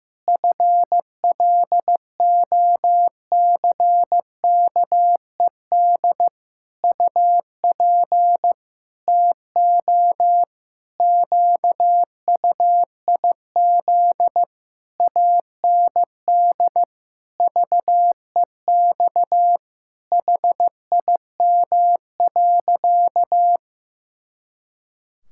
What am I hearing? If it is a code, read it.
Morse code, heard as a keyed tone, about 15 wpm, 700 Hz. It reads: FLOCKED UP TO QUIZ AND VEX HIM.
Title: Pangram Challenge v2 - 15wpm